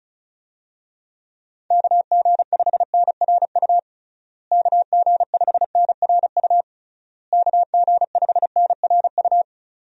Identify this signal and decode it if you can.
Morse code, heard as a keyed tone, about 36 wpm, 700 Hz. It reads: KG5NRU KG5NRU KG5NRU
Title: USA Call Signs - 6 Characters - Repeated 3x - 35wpm